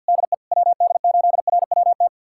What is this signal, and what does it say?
Morse code, heard as a keyed tone, about 46 wpm, 690 Hz. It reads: DE WD8RWT
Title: USA Callsign DE Callsign 50wpm